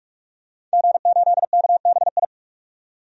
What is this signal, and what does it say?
Morse code, heard as a keyed tone, about 44 wpm, 690 Hz. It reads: G8KBI
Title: DXCC-100 Call Signs - 5 Characters - 45wpm